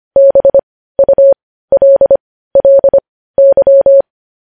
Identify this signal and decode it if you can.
Morse code, heard as a keyed tone, about 25 wpm, spaced as for 15 wpm, 550 Hz. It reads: BULLY
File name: podcast / News headlines-1x-25x15